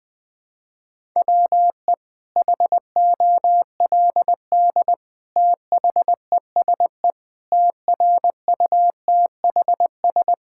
Morse code, heard as a keyed tone, about 20 wpm, 700 Hz. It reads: WE HOLD THESE TRUTHS
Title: Sets of 9 Words - 20wpm